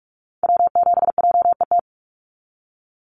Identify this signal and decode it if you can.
Morse code, heard as a keyed tone, about 44 wpm, 700 Hz. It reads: W7JET
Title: NA SOTA Activator Callsigns 45wpm